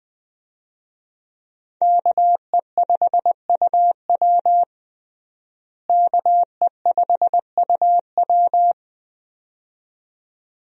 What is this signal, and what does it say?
Morse code, heard as a keyed tone, about 20 wpm, 700 Hz. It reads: KE5UW KE5UW
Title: USA Call Signs - 5 Characters - Repeated 2x - 20wpm